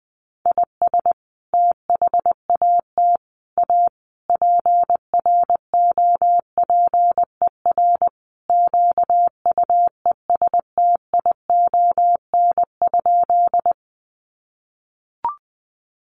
Morse code, heard as a keyed tone, about 20 wpm, 700 Hz. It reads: IS THAT A PROPER QUESTION?
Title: Sentences from Top 1100 Words 20wpm